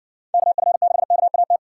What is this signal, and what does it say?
Morse code, heard as a keyed tone, about 57 wpm, 700 Hz. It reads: KF6KAT